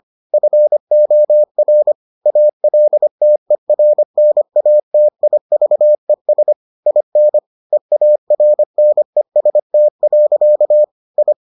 Morse code, heard as a keyed tone, about 25 wpm, 600 Hz. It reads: FOR ALTERNATIVES IN EARNEST. I